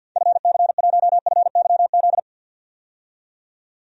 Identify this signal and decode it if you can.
Morse code, heard as a keyed tone, about 46 wpm, 700 Hz. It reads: FC1FYZ
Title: DXCC-100 Call Signs - 6 Characters - 50wpm